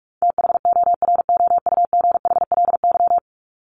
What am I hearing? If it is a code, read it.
Morse code, heard as a keyed tone, about 44 wpm, 700 Hz. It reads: THOROUGHLY
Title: LY Suffix - Top 100 Words with Base Words 45wpm